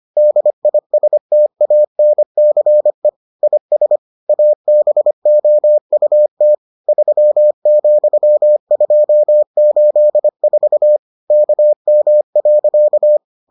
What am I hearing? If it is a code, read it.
Morse code, heard as a keyed tone, about 25 wpm, 600 Hz. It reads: DISTANCE IS ABOUT 3,284 KM.